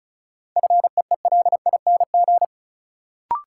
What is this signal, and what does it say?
Morse code, heard as a keyed tone, about 36 wpm, 700 Hz. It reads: FEELING
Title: Sentences from Top 1000 Words - Review Part 3 - New Word Drill - 35wpm